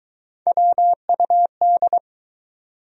Morse code, heard as a keyed tone, about 21 wpm, 700 Hz. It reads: WUD